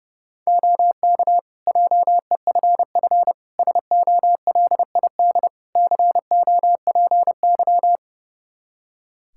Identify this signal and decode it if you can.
Morse code, heard as a keyed tone, about 30 wpm, 700 Hz. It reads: OK JEFF SOLID COPY
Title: K1USN SST QSO Practice 30wpm